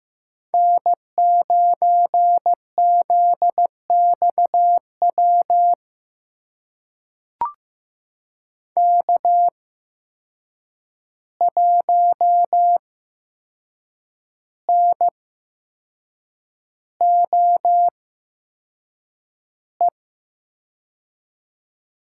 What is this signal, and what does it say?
Morse code, heard as a keyed tone, about 15 wpm, spaced as for 6 wpm, 700 Hz. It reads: N9ZXW K1NOE
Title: USA Call Signs - 5 Characters - 8x Character Spacing - 15wpm